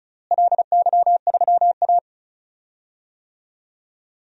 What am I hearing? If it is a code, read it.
Morse code, heard as a keyed tone, about 34 wpm, 700 Hz. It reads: LY3A
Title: DXCC-100 Call Signs - 4 Characters - Repeated 2x - 35wpm